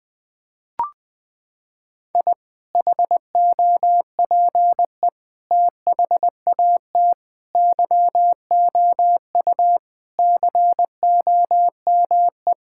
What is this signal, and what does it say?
Morse code, heard as a keyed tone, about 21 wpm, 700 Hz. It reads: I HOPE THAT YOU COME
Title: Sentences with 2-Word Phrases - 20wpm